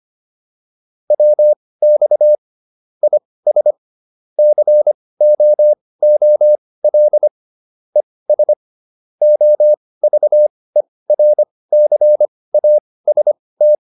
Morse code, heard as a keyed tone, about 25 wpm, spaced as for 18 wpm, 600 Hz. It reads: WX IS COOL ES OVERCAST